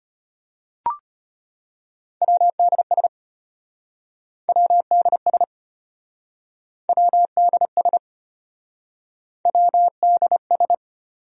Morse code, heard as a keyed tone, about 33 wpm, 700 Hz. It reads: WDS WDS WDS WDS